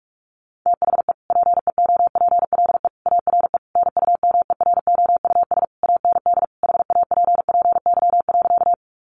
Morse code, heard as a keyed tone, about 44 wpm, 700 Hz. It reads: THE PEOPLE ARE NUMEROUS AND HAPPY.
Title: Sentences from Top 1400 Words 45wpm